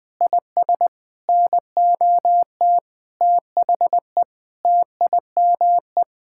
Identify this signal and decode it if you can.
Morse code, heard as a keyed tone, about 20 wpm, 700 Hz. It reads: IS NOT THE TIME
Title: Sentences from Top 1000 Words - Review Part 5 - 20wpm